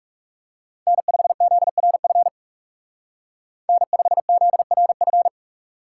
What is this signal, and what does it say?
Morse code, heard as a keyed tone, about 36 wpm, 700 Hz. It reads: N5ZRF N5ZRF
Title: Speed Racing - USA 1x3 Call Signs 30wpm